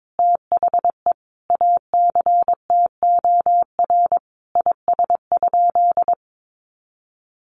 Morse code, heard as a keyed tone, about 22 wpm, 700 Hz. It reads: THE ACTOR IS?
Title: Sentences from Top 1500 Words 22wpm